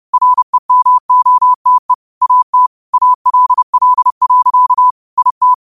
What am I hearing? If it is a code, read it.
Morse code, heard as a keyed tone, about 30 wpm, 1000 Hz. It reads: REMOTE AT ALL. IT